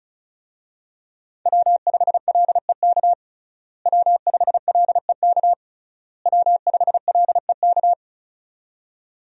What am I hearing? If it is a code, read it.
Morse code, heard as a keyed tone, about 34 wpm, 700 Hz. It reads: W5LEK W5LEK W5LEK
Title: USA Call Signs - 5 Characters - Repeated 3x - 35wpm